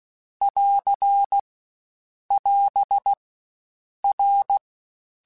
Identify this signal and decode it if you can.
Morse code, heard as a keyed tone, about 16 wpm, 800 Hz. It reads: <AR> <AS> R
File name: Combinations / Combinations-mp3s / Combinations@15